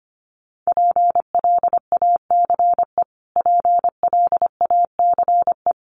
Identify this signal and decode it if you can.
Morse code, heard as a keyed tone, about 25 wpm, 700 Hz. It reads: PLACE PLACE
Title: Top 500 Words Repeated 25wpm